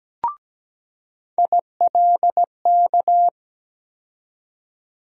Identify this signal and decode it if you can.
Morse code, heard as a keyed tone, about 17 wpm, 700 Hz. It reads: ILK
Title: USA Call Sign Postfixes - 3 Letters 17wpm